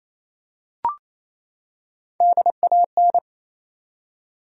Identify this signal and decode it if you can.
Morse code, heard as a keyed tone, about 28 wpm, 700 Hz. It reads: DAN
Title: USA Common Names - Max 5 Letters - 28wpm